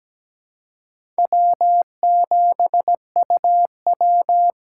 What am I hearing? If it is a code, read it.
Morse code, heard as a keyed tone, about 17 wpm, 700 Hz. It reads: W7UW
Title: USA Call Signs - 4 Characters - 4x Character Spacing - 17wpm